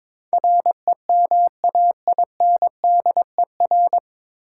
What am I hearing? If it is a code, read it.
Morse code, heard as a keyed tone, about 22 wpm, 700 Hz. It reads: REMAINDER
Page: Morse Code Ninja practice 22wpm